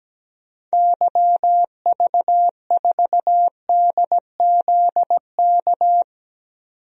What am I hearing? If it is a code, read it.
Morse code, heard as a keyed tone, about 17 wpm, 700 Hz. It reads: YV4DZK
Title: DXCC-100 Call Signs - 6 Characters - 17wpm